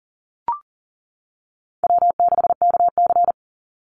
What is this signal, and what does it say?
Morse code, heard as a keyed tone, about 40 wpm, 700 Hz. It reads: W6KC
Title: Call Signs - 40wpm